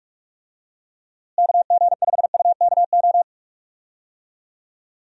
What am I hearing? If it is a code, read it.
Morse code, heard as a keyed tone, about 43 wpm, 700 Hz. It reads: KG5UKO
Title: USA Call Signs - 6 Characters - Repeated 3x - 45wpm